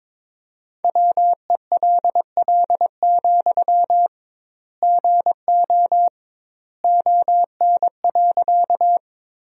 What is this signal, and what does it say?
Morse code, heard as a keyed tone, about 22 wpm, 700 Hz. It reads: WELL, GO ON.
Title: 2x Word-Spacing - Sentences from Top 200 Words - 22wpm